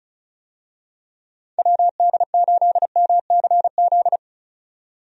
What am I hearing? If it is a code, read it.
Morse code, heard as a keyed tone, about 34 wpm, 700 Hz. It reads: WD8MCZ
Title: USA Call Signs - 6 Characters - 35wpm